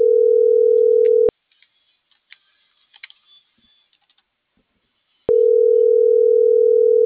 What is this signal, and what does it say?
Ringback tone (North America), 440 + 480 Hz.